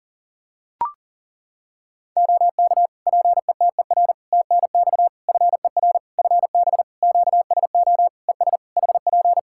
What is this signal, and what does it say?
Morse code, heard as a keyed tone, about 38 wpm, 700 Hz. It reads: OK PETER TNX FER FB QSO ES HP
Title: Practice QSO - Exchange 4 - Ending - 40wpm